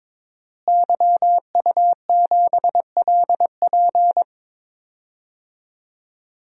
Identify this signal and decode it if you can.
Morse code, heard as a keyed tone, about 21 wpm, 700 Hz. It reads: YU7LP
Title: DXCC-100 Call Signs - 5 Characters - Repeated 2x - 22wpm